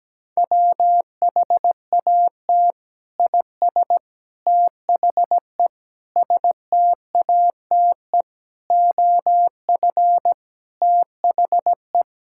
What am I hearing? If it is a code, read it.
Morse code, heard as a keyed tone, about 17 wpm, 700 Hz. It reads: WHAT IS THE STATE OF THE
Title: Sentences from Top 1000 Words - Review Part 5 - 17wpm